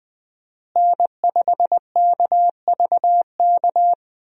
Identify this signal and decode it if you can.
Morse code, heard as a keyed tone, about 20 wpm, 700 Hz. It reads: N5KVK
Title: USA Call Signs - 5 Characters - Repeated 3x - 20wpm